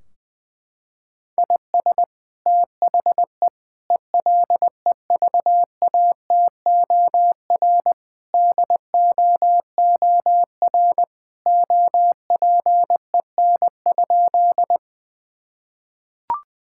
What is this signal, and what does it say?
Morse code, heard as a keyed tone, about 20 wpm, 700 Hz. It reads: IS THE ELEVATOR DOOR OPEN?
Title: Sentences from Top 1600 Words 20wpm